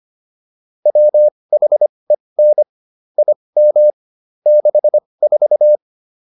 Morse code, heard as a keyed tone, about 25 wpm, spaced as for 20 wpm, 600 Hz. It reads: WHEN IM 64